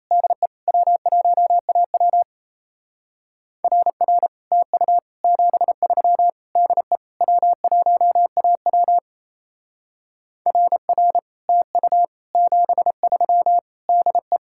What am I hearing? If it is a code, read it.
Morse code, heard as a keyed tone, about 34 wpm, 700 Hz. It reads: DE W1AW RR TU 73 DE W1AW RR TU 73 DE
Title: Speed-Racing Common QSO Phrases - 28wpm